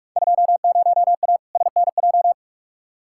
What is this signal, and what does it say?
Morse code, heard as a keyed tone, about 46 wpm, 700 Hz. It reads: 20A SNJ
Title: Field Day Exchanges Encore - 45wpm